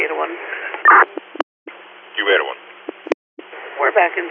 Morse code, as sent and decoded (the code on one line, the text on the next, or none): none